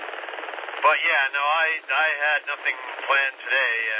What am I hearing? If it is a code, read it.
Emergency Alert System (SAME) tones, heard as harsh buzzing bursts, with speech. It is heard from 1.0 to 3.5 s.